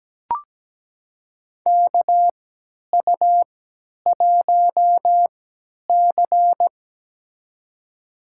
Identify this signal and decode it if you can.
Morse code, heard as a keyed tone, about 17 wpm, spaced as for 10 wpm, 700 Hz. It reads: KU1C
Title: USA Call Signs - 4 Characters - 3x Character Spacing - 17wpm